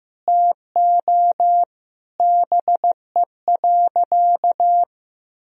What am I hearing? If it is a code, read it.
Morse code, heard as a keyed tone, about 15 wpm, 700 Hz. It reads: TO BE.
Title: Sentences from Top 1000 Words - Review Part 3 - 15wpm